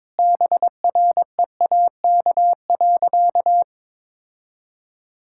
Morse code, heard as a keyed tone, about 22 wpm, 700 Hz. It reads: BREAK.
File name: practice / 22wpm / Sentences with 2-Word Phrases - 22wpm